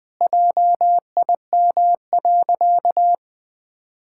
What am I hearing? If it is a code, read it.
Morse code, heard as a keyed tone, about 20 wpm, 700 Hz. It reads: JIM.
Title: Pangram Challenge v2 - 20wpm